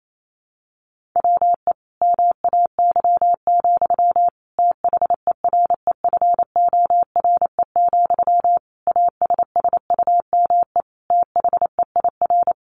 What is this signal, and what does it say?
Morse code, heard as a keyed tone, about 28 wpm, 700 Hz. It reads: WE MAY, THEREFORE, ASSUME THEIR
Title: Sentences from Top 1500 Words 28wpm